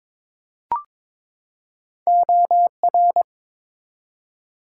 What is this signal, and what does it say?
Morse code, heard as a keyed tone, about 22 wpm, 700 Hz. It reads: OR
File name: ARRL Sections v2 - 22wpm